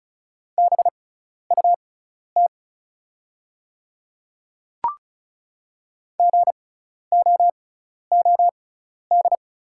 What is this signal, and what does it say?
Morse code, heard as a keyed tone, about 36 wpm, spaced as for 12 wpm, 700 Hz. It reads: BUT GOOD
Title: 6x Character Spacing - Top 300 Words - 35wpm